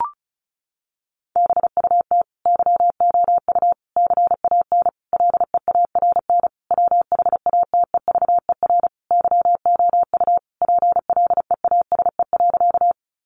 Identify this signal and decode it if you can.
Morse code, heard as a keyed tone, about 34 wpm, 700 Hz. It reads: BUT YOU CAN LEARN WHATEVER YOU PLEASE.
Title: Sentences from Top 1500 Words 35wpm